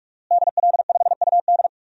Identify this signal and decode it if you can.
Morse code, heard as a keyed tone, about 46 wpm, 700 Hz. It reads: DL5UD
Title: DXCC-100 Call Signs - 5 Characters - Repeated 2x - 45wpm